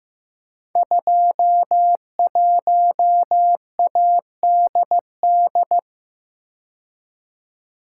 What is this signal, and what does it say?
Morse code, heard as a keyed tone, about 15 wpm, 700 Hz. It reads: 21ADD
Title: Code Groups - 5 Letter:Numbers - 15wpm